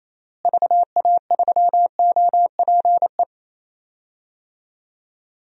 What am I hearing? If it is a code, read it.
Morse code, heard as a keyed tone, about 28 wpm, 700 Hz. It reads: VA3OPE
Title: DXCC-100 Call Signs - 6 Characters - Repeated 2x - 28wpm